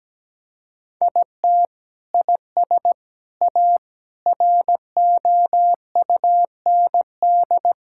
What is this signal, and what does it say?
Morse code, heard as a keyed tone, about 17 wpm, 700 Hz. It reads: IT IS A ROUND
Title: Sentences from Top 1000 Words - Review Part 6 - 17wpm